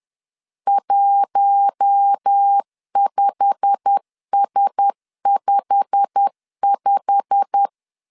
Morse code, heard as a keyed tone, about 10 wpm, 800 Hz. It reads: 15S55
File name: LEZIONI / 8 YJ45 / 8 YJ45 10wpm 2549